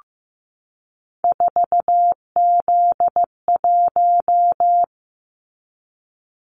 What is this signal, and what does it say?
Morse code, heard as a keyed tone, about 15 wpm, 700 Hz. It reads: 4Z1
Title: Callsign Prefixes for Top 50 DXCC - 15wpm